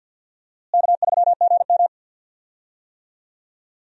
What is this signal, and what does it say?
Morse code, heard as a keyed tone, about 52 wpm, 700 Hz. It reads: K3GM
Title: USA Call Signs - 4 Characters - Repeated 2x - 50wpm